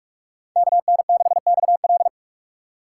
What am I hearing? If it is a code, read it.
Morse code, heard as a keyed tone, about 46 wpm, 690 Hz. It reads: KN6XL
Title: USA Call Signs - 4x Character Spacing - 45wpm